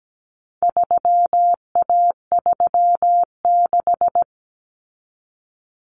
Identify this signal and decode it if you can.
Morse code, heard as a keyed tone, about 17 wpm, 700 Hz. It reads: 3A36
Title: Common Cut Numbers - 4-Digit 17wpm